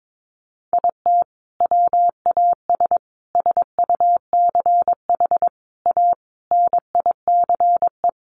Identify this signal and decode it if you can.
Morse code, heard as a keyed tone, about 22 wpm, 700 Hz. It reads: IT WAS SUCH A NICE
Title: Sentences from Top 1400 Words 22wpm